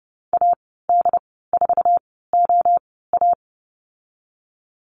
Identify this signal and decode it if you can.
Morse code, heard as a keyed tone, about 30 wpm, spaced as for 17 wpm, 700 Hz. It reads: AD4OA
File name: Lesson 15g - Number 5 - Call Signs 30wpm